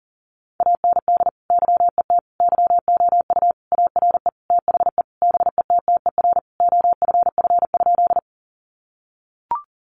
Morse code, heard as a keyed tone, about 40 wpm, 700 Hz. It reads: AND YET YOU ARE THE BETTER OFF?
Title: Sentences from Top 500 Words 40wpm